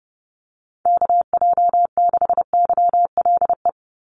Morse code, heard as a keyed tone, about 30 wpm, 700 Hz. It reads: KJ6YLE
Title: USA Call Signs 30wpm